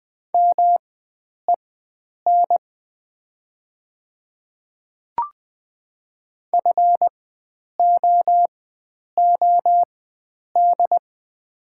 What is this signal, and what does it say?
Morse code, heard as a keyed tone, about 20 wpm, spaced as for 9 wpm, 700 Hz. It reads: MEN FOOD
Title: 4x Character Spacing - Top 300 Words - 20wpm